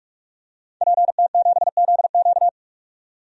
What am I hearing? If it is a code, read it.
Morse code, heard as a keyed tone, about 42 wpm, 700 Hz. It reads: PT7ZQ
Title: DXCC-100 Call Signs - 5 Characters - 45wpm